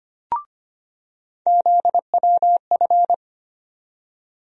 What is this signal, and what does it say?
Morse code, heard as a keyed tone, about 25 wpm, 700 Hz. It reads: ZWF
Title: USA Call Sign Postfixes - 3 Letters 25wpm